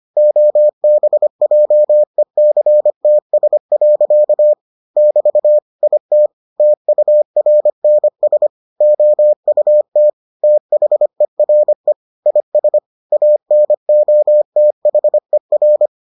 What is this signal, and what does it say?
Morse code, heard as a keyed tone, about 25 wpm, 600 Hz. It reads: OBJECTS. <BT> IT TURNS OUT THERE IS ANOTHER